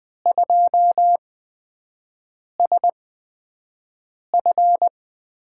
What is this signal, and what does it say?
Morse code, heard as a keyed tone, about 20 wpm, spaced as for 5 wpm, 700 Hz. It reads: 2SF
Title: USA Call Signs - 6 Characters - 8x Character Spacing - 20wpm